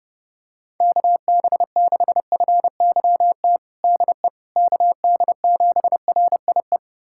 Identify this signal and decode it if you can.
Morse code, heard as a keyed tone, about 30 wpm, 700 Hz. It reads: KB6FYT DE KD7RIE
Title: USA Callsign DE Callsign - 6 Characters - 30wpm